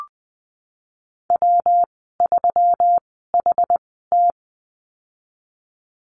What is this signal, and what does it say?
Morse code, heard as a keyed tone, about 20 wpm, spaced as for 14 wpm, 700 Hz. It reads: W3HT
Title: 128 Lesson 21g - Number 6 - Call Signs 20wpm